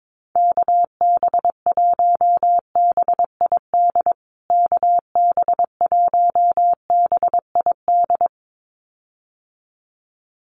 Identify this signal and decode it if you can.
Morse code, heard as a keyed tone, about 22 wpm, 700 Hz. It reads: KB1BID KB1BID
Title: USA Call Signs Repeated 2x 22wpm